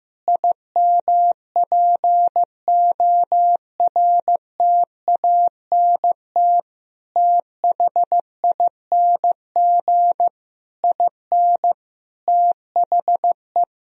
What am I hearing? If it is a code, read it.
Morse code, heard as a keyed tone, about 15 wpm, 700 Hz. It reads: IMPORTANT THING IN THE
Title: Sentences from Top 1000 Words - Review Part 1 - 15wpm